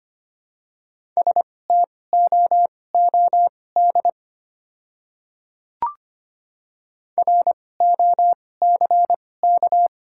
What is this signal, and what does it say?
Morse code, heard as a keyed tone, about 25 wpm, spaced as for 18 wpm, 700 Hz. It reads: STOOD ROCK
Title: 2x Character Spacing - Top 500 Words - 25wpm